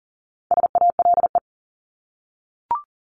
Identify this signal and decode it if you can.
Morse code, heard as a keyed tone, about 40 wpm, 700 Hz. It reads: SALE
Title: Top 1251-1500 Words 40wpm